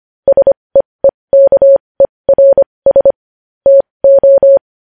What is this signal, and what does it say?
Morse code, heard as a keyed tone, about 25 wpm, spaced as for 20 wpm, 550 Hz. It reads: SEEKERS TO
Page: news-headlines-in-morse-code-at-20-wpm